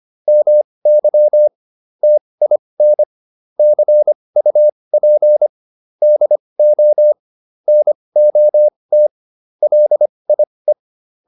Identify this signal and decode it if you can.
Morse code, heard as a keyed tone, about 25 wpm, spaced as for 20 wpm, 600 Hz. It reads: MY TIN CUP DO NOT LIE